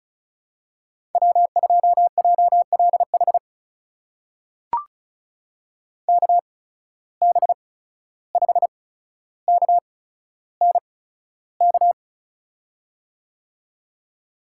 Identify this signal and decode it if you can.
Morse code, heard as a keyed tone, about 36 wpm, spaced as for 14 wpm, 700 Hz. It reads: W2JLH KB5KNK
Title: USA Call Signs - 8x Character Spacing - 35wpm